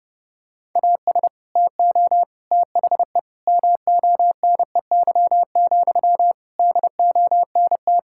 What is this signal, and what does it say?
Morse code, heard as a keyed tone, about 30 wpm, 700 Hz. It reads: AS TO THE MONEY, DONT
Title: Sentences from Top 1000 Words - Review Part 5 - 30wpm